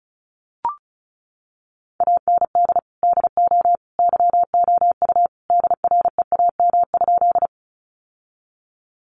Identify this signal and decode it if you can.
Morse code, heard as a keyed tone, about 34 wpm, 700 Hz. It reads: AND DO YOU DREAM?
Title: Sentences from Top 1500 Words 35wpm